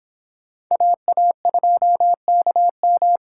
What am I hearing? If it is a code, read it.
Morse code, heard as a keyed tone, about 26 wpm, 700 Hz. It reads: AA2KM